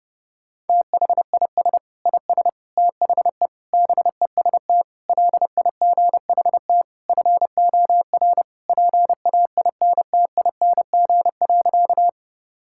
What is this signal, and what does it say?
Morse code, heard as a keyed tone, about 32 wpm, 700 Hz. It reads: THIS IS THE BEST LIGHT FOR PAINTING.
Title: Sentences from Top 1000 Words - Review Part 6 - 30wpm